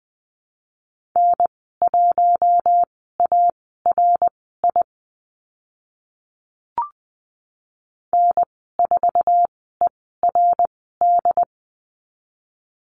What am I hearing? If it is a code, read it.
Morse code, heard as a keyed tone, about 20 wpm, spaced as for 15 wpm, 700 Hz. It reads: N1ARI N4ERD
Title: 069 Lesson 12g - Letter d - Call Signs 20wpm